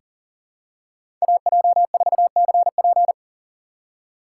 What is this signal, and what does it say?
Morse code, heard as a keyed tone, about 40 wpm, 700 Hz. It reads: AJ4CP
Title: USA Call Signs - 5 Characters - 2x Character Spacing - 40wpm